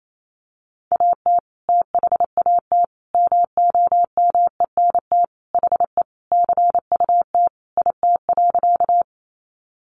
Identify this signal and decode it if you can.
Morse code, heard as a keyed tone, about 28 wpm, 700 Hz. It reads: AT THAT MOMENT HE CUT IT.
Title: Sentences from Top 1000 Words 28wpm